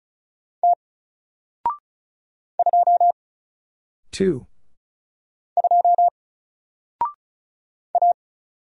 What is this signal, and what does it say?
Morse code, heard as a keyed tone, about 35 wpm, 700 Hz. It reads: T 2 2 A
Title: Single Letter-Number 35wpm